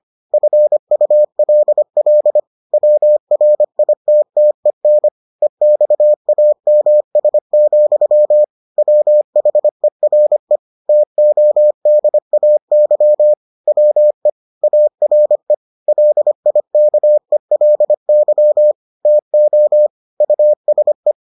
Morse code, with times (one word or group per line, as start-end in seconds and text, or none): 0.3-2.4 FULL
2.7-5.1 WRITTEN
5.4-8.4 EXAMS,
8.8-10.6 WHERE
10.9-13.3 TODAY
13.7-14.3 WE
14.6-15.5 ARE
15.9-18.7 LIKELY
19.1-19.9 TO
20.2-21.1 USE